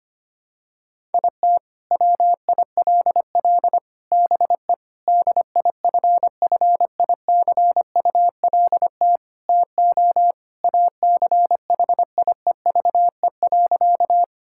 Morse code, heard as a keyed tone, about 25 wpm, 700 Hz. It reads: IT WILL BE DIFFICULT TO ACHIEVE.
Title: Sentences from Top 1000 Words - Review Part 6 - 25wpm